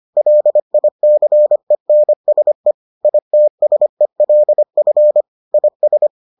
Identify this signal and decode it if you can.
Morse code, heard as a keyed tone, about 25 wpm, 600 Hz. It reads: LICENSE ITSELF IS